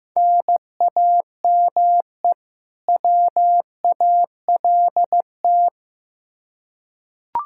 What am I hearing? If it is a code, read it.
Morse code, heard as a keyed tone, about 15 wpm, 700 Hz. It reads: NAME WALT
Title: Practice QSO - Exchange 1 - Beginning - 15wpm